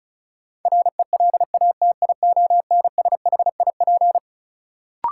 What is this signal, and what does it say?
Morse code, heard as a keyed tone, about 36 wpm, 700 Hz. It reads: RELATIONSHIP
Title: Sentences from Top 1100 Words - New Word Drill - 35wpm